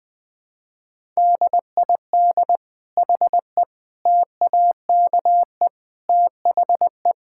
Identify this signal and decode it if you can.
Morse code, heard as a keyed tone, about 21 wpm, 700 Hz. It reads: DID HE TAKE THE
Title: Sentences from Top 1000 Words - Review Part 6 - 20wpm